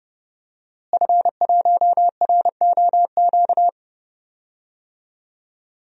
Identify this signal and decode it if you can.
Morse code, heard as a keyed tone, about 30 wpm, 700 Hz. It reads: F1ROQ
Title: DXCC-100 Call Signs - 5 Characters - Repeated 3x - 30wpm